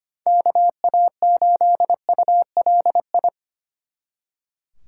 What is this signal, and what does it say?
Morse code, heard as a keyed tone, about 29 wpm, 700 Hz. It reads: KA8ULI
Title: Speed Racing - USA 2x3 Call Signs 25wpm